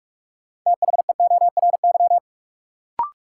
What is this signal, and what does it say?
Morse code, heard as a keyed tone, about 46 wpm, 700 Hz. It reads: THEORY
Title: Sentences from Top 600 Words - New Word Drill - 45wpm